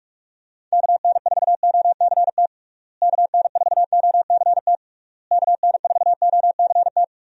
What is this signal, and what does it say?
Morse code, heard as a keyed tone, about 46 wpm, 700 Hz. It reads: KN4OCT KN4OCT KN4OCT
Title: USA Call Signs - 6 Characters - Repeated 3x - 45wpm